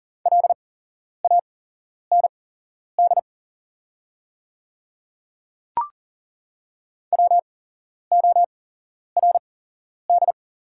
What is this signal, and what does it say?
Morse code, heard as a keyed tone, about 40 wpm, spaced as for 11 wpm, 690 Hz. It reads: LAND WORD
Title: 8x Character Spacing - Top 200 Words - 40wpm